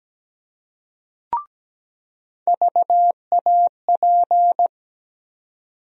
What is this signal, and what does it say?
Morse code, heard as a keyed tone, about 17 wpm, 700 Hz. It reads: VAP
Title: USA Call Sign Postfixes - 3 Letters 17wpm